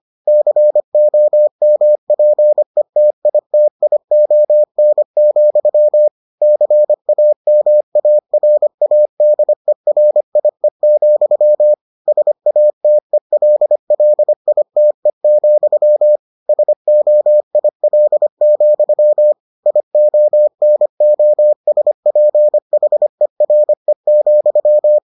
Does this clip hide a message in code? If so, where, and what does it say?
Yes: Morse code from 0.3 to 25.0 s, heard as a keyed tone, about 25 wpm, 600 Hz. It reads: COMPETITION, CAMARADERIE, SATELLITE, SOIL, IONOSPHERE,